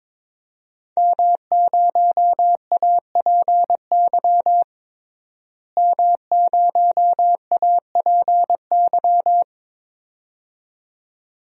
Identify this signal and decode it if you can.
Morse code, heard as a keyed tone, about 22 wpm, 700 Hz. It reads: M0APY M0APY
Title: DXCC-100 Call Signs - 5 Characters - Repeated 2x - 22wpm